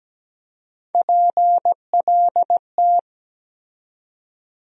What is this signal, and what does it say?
Morse code, heard as a keyed tone, about 17 wpm, 700 Hz. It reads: PLT
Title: USA Call Sign Postfixes - 3 Letters 17wpm